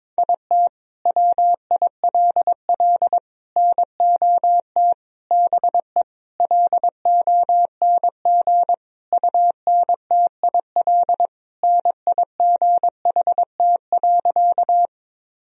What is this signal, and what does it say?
Morse code, heard as a keyed tone, about 21 wpm, 700 Hz. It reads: IT WILL NOT BE LONG UNTIL NIGHT.
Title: Sentences from Top 1000 Words - Review Part 1 - 22wpm